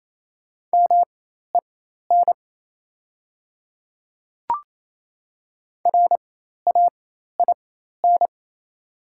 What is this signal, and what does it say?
Morse code, heard as a keyed tone, about 28 wpm, spaced as for 13 wpm, 700 Hz. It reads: MEN RAIN
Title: 4x Character Spacing - Top 300 Words - 28wpm